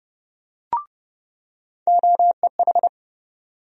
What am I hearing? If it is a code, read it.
Morse code, heard as a keyed tone, about 31 wpm, 700 Hz. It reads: OEH
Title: USA Call Sign Postfixes - 3 Letters 30wpm